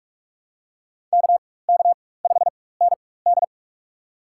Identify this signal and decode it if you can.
Morse code, heard as a keyed tone, about 46 wpm, spaced as for 21 wpm, 700 Hz. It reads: KK5ND
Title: USA Call Signs - 5 Characters - 4x Character Spacing - 45wpm